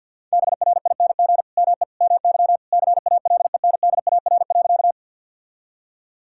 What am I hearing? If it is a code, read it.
Morse code, heard as a keyed tone, about 50 wpm, 700 Hz. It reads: BRING ME MY CALENDAR.
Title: Sentences from Top 1600 Words 50wpm